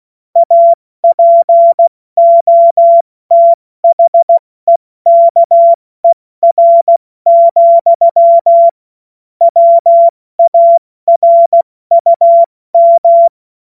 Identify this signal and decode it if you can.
Morse code, heard as a keyed tone, about 16 wpm, 680 Hz. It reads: APOTHEKER, WARUM
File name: Morsekurs / Witze / Witz 02 Januar 2024 70 F80